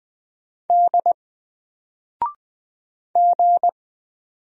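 Morse code, heard as a keyed tone, about 20 wpm, 700 Hz. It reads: D G
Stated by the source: Single letter and number Morse drill at 20 wpm, from N4LCD.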